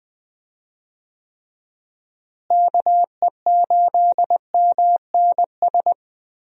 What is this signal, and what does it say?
Morse code, heard as a keyed tone, about 20 wpm, 700 Hz. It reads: KE8MNS
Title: USA Call Signs - 6 Characters - Repeated 3x - 20wpm